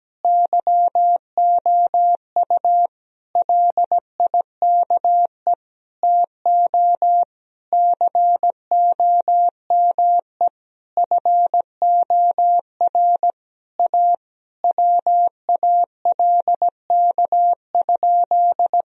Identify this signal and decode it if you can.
Morse code, heard as a keyed tone, about 17 wpm, 700 Hz. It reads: YOU LIKE TO COME FOR A WALK?
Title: Sentences from Top 1000 Words - Review Part 1 - 17wpm